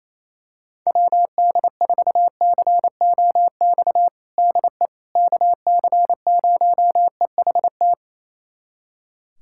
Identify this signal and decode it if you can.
Morse code, heard as a keyed tone, about 28 wpm, 700 Hz. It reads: WD4COX DE KC0EHT
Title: USA Callsign DE Callsign 28wpm